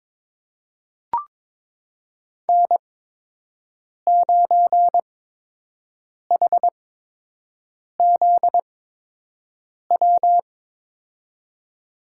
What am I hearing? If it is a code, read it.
Morse code, heard as a keyed tone, about 21 wpm, spaced as for 6 wpm, 700 Hz. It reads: N9HZW
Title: USA Call Signs - 5 Characters - 8x Character Spacing - 22wpm